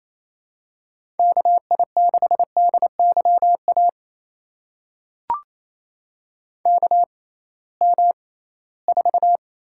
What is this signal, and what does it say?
Morse code, heard as a keyed tone, about 28 wpm, spaced as for 18 wpm, 700 Hz. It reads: KI6DYA KM4
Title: USA Call Signs - 6 Characters - 6x Character Spacing - 28wpm